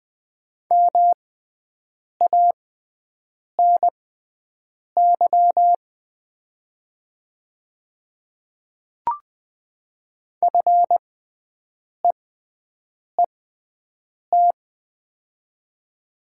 Morse code, heard as a keyed tone, about 20 wpm, spaced as for 7 wpm, 690 Hz. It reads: MANY FEET
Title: 6x Character Spacing - Top 500 Words - 20wpm